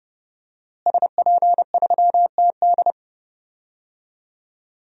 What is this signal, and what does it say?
Morse code, heard as a keyed tone, about 30 wpm, 700 Hz. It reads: SP3TD
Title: DXCC-100 Call Signs - 5 Characters - Repeated 3x - 30wpm